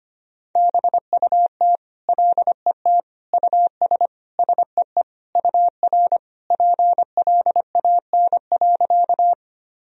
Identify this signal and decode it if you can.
Morse code, heard as a keyed tone, about 25 wpm, 700 Hz. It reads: BUT LET US SEE UR PLAN.
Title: Sentences from Top 500 Words and Abbreviations 25wpm